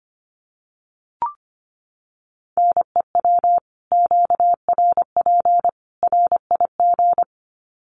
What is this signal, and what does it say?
Morse code, heard as a keyed tone, about 25 wpm, 700 Hz. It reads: NEW QRP RIG
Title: Common QSO Phrases 25wpm